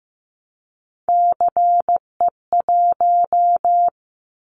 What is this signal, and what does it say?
Morse code, heard as a keyed tone, about 15 wpm, 700 Hz. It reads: CE1
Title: Callsign Prefixes for Top 100 DXCC - 15wpm